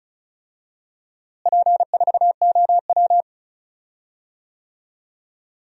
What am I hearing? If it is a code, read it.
Morse code, heard as a keyed tone, about 36 wpm, 700 Hz. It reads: P4OW
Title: DXCC-100 Call Signs - 4 Characters - Repeated 2x - 35wpm